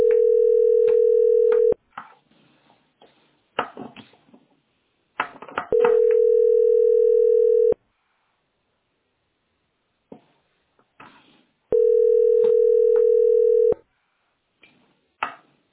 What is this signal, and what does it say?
Ringback tone (North America), 440 + 480 Hz.